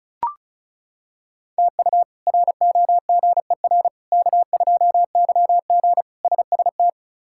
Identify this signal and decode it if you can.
Morse code, heard as a keyed tone, about 36 wpm, 700 Hz. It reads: TU ROGER K2YG SST
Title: K1USN SST QSO Practice - 35wpm